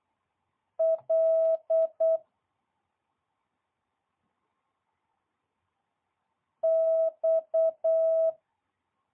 Morse code, heard as a keyed tone, about 8 wpm, 650 Hz. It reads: L X
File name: jh8cbh / 2022cw